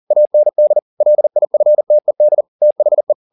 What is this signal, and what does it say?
Morse code, heard as a keyed tone, about 40 wpm, 600 Hz. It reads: AND LIFTED THE
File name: the old man and the sea 2 40wpm